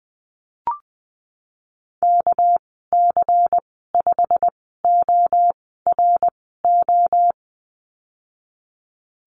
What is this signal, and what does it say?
Morse code, heard as a keyed tone, about 20 wpm, spaced as for 15 wpm, 700 Hz. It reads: KC5ORO
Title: 219 Lesson 35g - Number 0 - Call Signs 20wpm